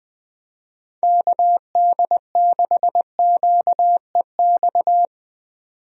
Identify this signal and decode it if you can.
Morse code, heard as a keyed tone, about 21 wpm, 700 Hz. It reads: KD6QEX
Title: USA Call Signs - 6 Characters - 20wpm